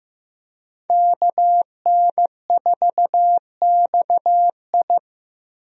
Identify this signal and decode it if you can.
Morse code, heard as a keyed tone, about 15 wpm, 700 Hz. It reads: KN4XI
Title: USA Call Signs - 5 Characters - Repeated 3x - 15wpm